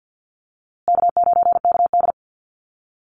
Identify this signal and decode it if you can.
Morse code, heard as a keyed tone, about 48 wpm, 700 Hz. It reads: K9KD
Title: USA Call Signs Repeated 3x 50wpm